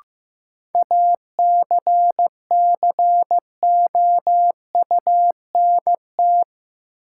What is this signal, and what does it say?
Morse code, heard as a keyed tone, about 15 wpm, 700 Hz. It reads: ACCOUNT
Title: Sentences from Top 900 Words - New Word Drill - 15wpm